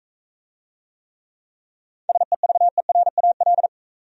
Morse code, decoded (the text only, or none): SEVERAL